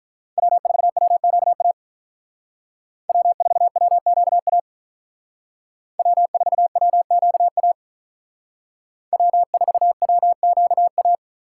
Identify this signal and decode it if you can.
Morse code, heard as a keyed tone, about 45 wpm, 700 Hz. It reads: W4WQA W4WQA W4WQA W4WQA